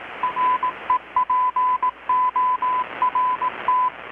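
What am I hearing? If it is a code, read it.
Morse code, heard as a keyed tone, about 18 wpm, 1000 Hz, with radio noise. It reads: REPORT
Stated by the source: Morse from KPH's Night of Nights XX maritime radiotelegraphy event, received in Chiapas, Mexico about 3,800 km away on a Kenwood R-600 with a grounded 10 m long-wire antenna.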